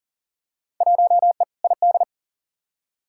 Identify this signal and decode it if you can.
Morse code, heard as a keyed tone, about 40 wpm, 700 Hz. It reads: 1E ID
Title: Field Day Exchanges - 40wpm